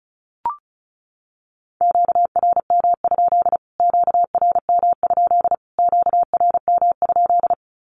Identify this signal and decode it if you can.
Morse code, heard as a keyed tone, about 34 wpm, 700 Hz. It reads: QRM? QRM? QRM?
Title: Q-Codes Repeated 3x 35wpm